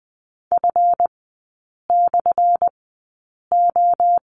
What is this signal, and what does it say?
Morse code, heard as a keyed tone, about 20 wpm, spaced as for 8 wpm, 700 Hz. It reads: F/O
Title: 184 Lesson 30e - Letter b - Three Characters 20wpm